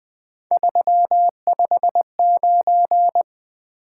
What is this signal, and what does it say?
Morse code, heard as a keyed tone, about 25 wpm, 700 Hz. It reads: 359
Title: Speed-Racing Top 180 QSO Elements 20wpm